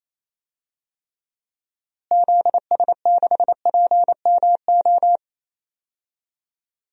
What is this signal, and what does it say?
Morse code, heard as a keyed tone, about 28 wpm, 700 Hz. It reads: ZS6PMO
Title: DXCC-100 Call Signs - 6 Characters - Repeated 2x - 28wpm